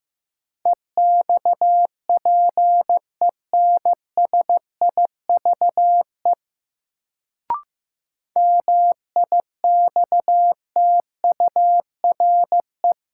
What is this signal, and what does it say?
Morse code, heard as a keyed tone, about 15 wpm, 700 Hz. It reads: EXPENSIVE MIXTURE
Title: Sentences from Top 1500 Words - New Word Drill Alt - 15wpm